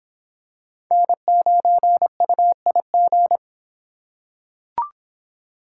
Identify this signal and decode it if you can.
Morse code, heard as a keyed tone, about 23 wpm, 700 Hz. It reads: N9UIG